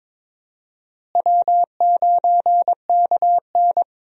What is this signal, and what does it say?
Morse code, heard as a keyed tone, about 22 wpm, 700 Hz. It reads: W9KN
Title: USA Call Signs - 4 Characters - Repeated 2x - 22wpm